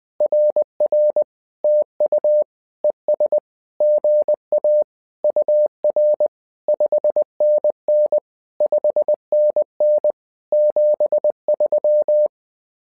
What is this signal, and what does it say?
Morse code, heard as a keyed tone, about 20 wpm, 600 Hz. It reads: RR TU ES GA UR 5NN 5NN 73
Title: SOTA QSO Practice 20wpm